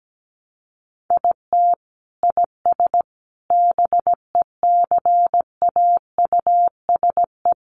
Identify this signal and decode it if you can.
Morse code, heard as a keyed tone, about 17 wpm, 700 Hz. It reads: IT IS BECAUSE
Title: Sentences from Top 600 Words 17wpm